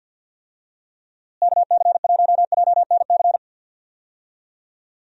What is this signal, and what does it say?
Morse code, heard as a keyed tone, about 47 wpm, 700 Hz. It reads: KC1JNC
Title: USA Call Signs - 6 Characters - Repeated 2x - 50wpm